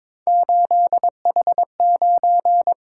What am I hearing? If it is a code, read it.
Morse code, heard as a keyed tone, about 23 wpm, 700 Hz. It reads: 8H9
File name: practice / 22wpm / Three Letter-Number Combinations - 22wpm